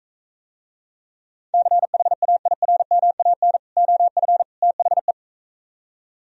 Morse code, heard as a keyed tone, about 44 wpm, 700 Hz. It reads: CHAIRMAN OF THE